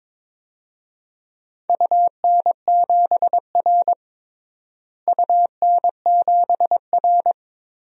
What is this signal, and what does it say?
Morse code, heard as a keyed tone, about 22 wpm, 700 Hz. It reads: UN7R UN7R
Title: DXCC-100 Call Signs - 4 Characters - Repeated 2x - 22wpm